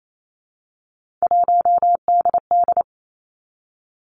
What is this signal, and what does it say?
Morse code, heard as a keyed tone, about 28 wpm, 700 Hz. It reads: 1DD
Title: Three Challenging Letter-Number Combinations 28wpm